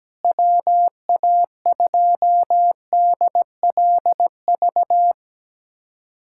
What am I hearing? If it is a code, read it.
Morse code, heard as a keyed tone, about 17 wpm, 700 Hz. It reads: WA2DLV
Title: USA Callsign DE Callsign - 6 Characters - 17wpm